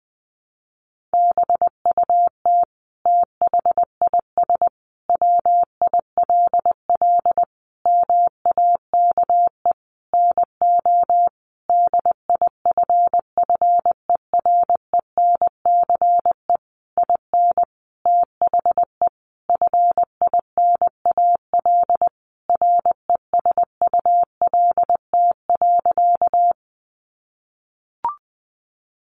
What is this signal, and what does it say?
Morse code, heard as a keyed tone, about 20 wpm, 700 Hz. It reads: BUT THIS WILL MAKE NO DIFFERENCE IN THE FINAL RESULT.
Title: Sentences from Top 800 Words 20wpm